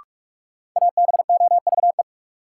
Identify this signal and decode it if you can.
Morse code, heard as a keyed tone, about 44 wpm, 700 Hz. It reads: ABOVE
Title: Sentences from Top 1200 Words - New Word Drill Alt - 45wpm